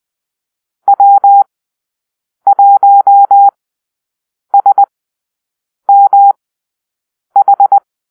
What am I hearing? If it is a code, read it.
Morse code, heard as a keyed tone, about 20 wpm, spaced as for 7 wpm, 800 Hz. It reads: W1SMH
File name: arrl / Call-Sign Drill-8